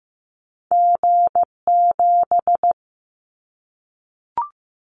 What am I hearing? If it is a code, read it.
Morse code, heard as a keyed tone, about 15 wpm, 700 Hz. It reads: G7
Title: Callsign Prefixes for Top 25 DXCC - Repeated 3x - 15wpm